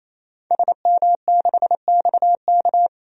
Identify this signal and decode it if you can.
Morse code, heard as a keyed tone, about 28 wpm, 700 Hz. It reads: SM6XK
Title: DXCC-100 Call Signs - 5 Characters - 28wpm